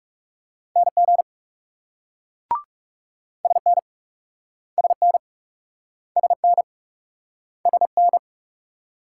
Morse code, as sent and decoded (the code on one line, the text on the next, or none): -. --. / ... -. / ... -. / ... -. / ... -.
NG SN SN SN SN